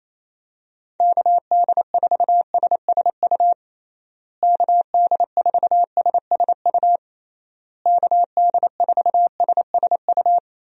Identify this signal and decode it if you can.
Morse code, heard as a keyed tone, about 28 wpm, 700 Hz. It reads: KD4SSU KD4SSU KD4SSU
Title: USA Call Signs - 6 Characters - Repeated 3x - 28wpm